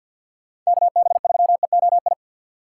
Morse code, heard as a keyed tone, about 50 wpm, 700 Hz. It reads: KB3EOI
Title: USA Call Signs - 6x Character Spacing - 50wpm